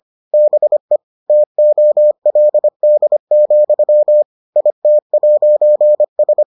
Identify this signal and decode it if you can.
Morse code, heard as a keyed tone, about 25 wpm, 600 Hz. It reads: BE TOLD, IT'S